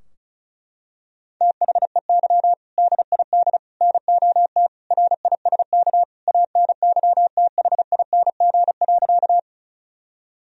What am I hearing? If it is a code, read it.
Morse code, heard as a keyed tone, about 36 wpm, 700 Hz. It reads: THEY DID NOT RISK ANYTHING.
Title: Sentences from Top 1000 Words - Review Part 4 - 35wpm